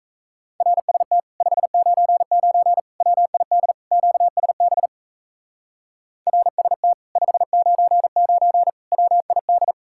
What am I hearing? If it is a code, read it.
Morse code, heard as a keyed tone, about 34 wpm, 700 Hz. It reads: RST 599 WID QSB RST 599 WID
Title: Speed-Racing Common QSO Phrases Encore - 28wpm